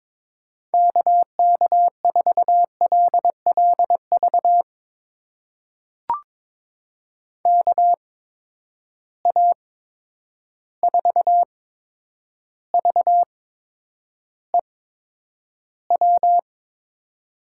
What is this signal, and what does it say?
Morse code, heard as a keyed tone, about 22 wpm, spaced as for 9 wpm, 700 Hz. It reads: KK4LLV KA4VEW
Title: USA Call Signs - 6 Characters - 8x Character Spacing - 22wpm